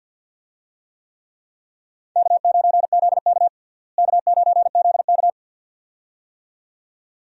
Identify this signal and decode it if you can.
Morse code, heard as a keyed tone, about 47 wpm, 700 Hz. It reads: K9ZK K9ZK
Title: USA Call Signs - 4 Characters - Repeated 2x - 50wpm